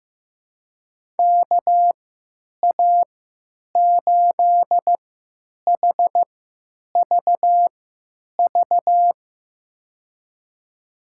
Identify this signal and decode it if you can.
Morse code, heard as a keyed tone, about 15 wpm, spaced as for 9 wpm, 700 Hz. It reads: KA8HVV
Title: USA Call Signs - 3x Character Spacing - 15wpm